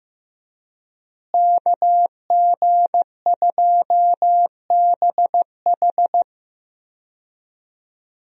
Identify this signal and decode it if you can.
Morse code, heard as a keyed tone, about 15 wpm, 700 Hz. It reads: KG2BH
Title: USA Call Signs - 5 Characters - Repeated 3x - 15wpm